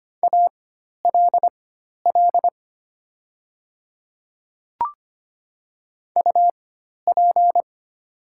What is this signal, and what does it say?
Morse code, heard as a keyed tone, about 25 wpm, spaced as for 12 wpm, 700 Hz. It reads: ALL UP
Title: 4x Character Spacing - Top 100 Words - 25wpm